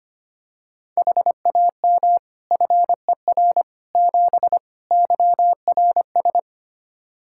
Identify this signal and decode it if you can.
Morse code, heard as a keyed tone, about 25 wpm, 700 Hz. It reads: HAM FER 7 YRS
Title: Practice QSO - Exchange 3 - More Info - 25wpm